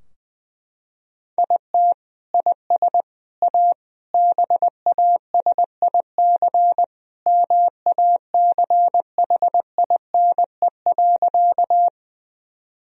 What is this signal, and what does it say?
Morse code, heard as a keyed tone, about 20 wpm, 700 Hz. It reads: IT IS A BASIC MACHINE.
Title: Sentences from Top 1000 Words - Review Part 3 - 20wpm